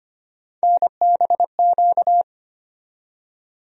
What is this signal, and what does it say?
Morse code, heard as a keyed tone, about 25 wpm, 700 Hz. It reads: NBQ
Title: USA Call Sign Postfixes - 3 Letters 25wpm